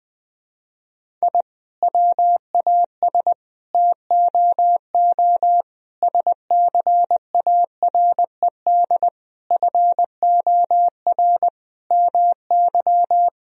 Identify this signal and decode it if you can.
Morse code, heard as a keyed tone, about 21 wpm, 700 Hz. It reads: I WAS TOO SCARED FOR MY
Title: Sentences from Top 1000 Words - Review Part 4 - 20wpm